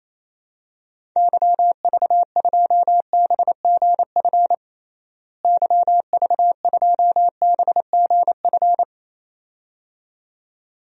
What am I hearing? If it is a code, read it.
Morse code, heard as a keyed tone, about 28 wpm, 700 Hz. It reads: YV2BGF YV2BGF
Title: DXCC-100 Call Signs - 6 Characters - Repeated 2x - 28wpm